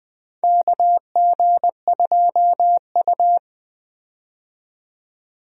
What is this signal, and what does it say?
Morse code, heard as a keyed tone, about 20 wpm, 700 Hz. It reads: KG2U
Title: USA Call Signs - 4 Characters - 20wpm